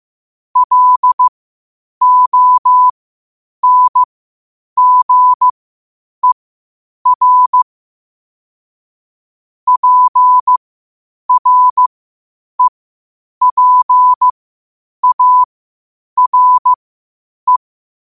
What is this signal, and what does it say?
Morse code, heard as a keyed tone, about 15 wpm, spaced as for 8 wpm, 1000 Hz. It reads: LONGER PREPARE